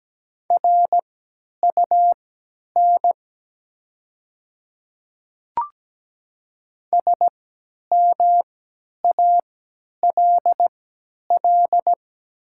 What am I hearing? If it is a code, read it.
Morse code, heard as a keyed tone, about 17 wpm, spaced as for 10 wpm, 690 Hz. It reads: RUN SMALL
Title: 3x Character Spacing - Top 300 Words - 17wpm